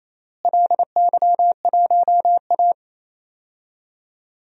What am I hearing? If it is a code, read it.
Morse code, heard as a keyed tone, about 28 wpm, 700 Hz. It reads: LY1A
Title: DXCC-100 Call Signs - 4 Characters - Repeated 3x - 28wpm